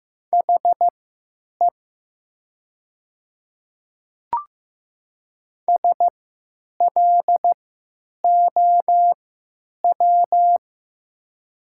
Morse code, heard as a keyed tone, about 15 wpm, spaced as for 9 wpm, 700 Hz. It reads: HE SLOW
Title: 3x Character Spacing - Top 500 Words - 15wpm